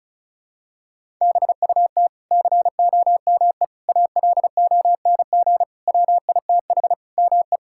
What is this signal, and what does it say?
Morse code, heard as a keyed tone, about 36 wpm, 700 Hz. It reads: BUT COME ALONG WITH ME
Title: Sentences from Top 1000 Words - Review Part 6 - 35wpm